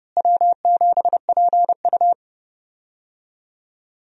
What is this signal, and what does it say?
Morse code, heard as a keyed tone, about 30 wpm, 700 Hz. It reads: W7PU
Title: USA Call Signs - 4 Characters - Repeated 2x - 30wpm